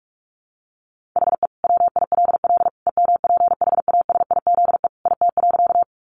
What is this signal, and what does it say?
Morse code, heard as a keyed tone, about 44 wpm, 700 Hz. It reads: HE WILL EMPHASIZE IT.
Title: Sentences from Top 1500 Words 45wpm